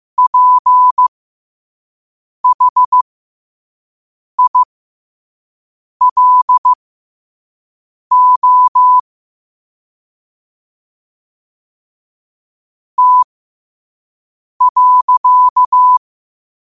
Morse code, heard as a keyed tone, about 15 wpm, spaced as for 5 wpm, 1000 Hz. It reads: PHILO T.
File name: 5wpm